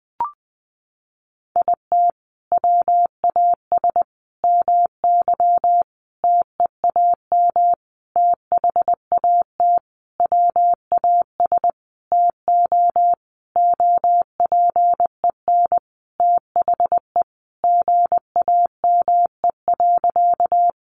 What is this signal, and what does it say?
Morse code, heard as a keyed tone, about 20 wpm, 700 Hz. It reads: IT WAS MY TEAM THAT WAS TO OPEN THE GAME.
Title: Sentences from Top 1500 Words 20wpm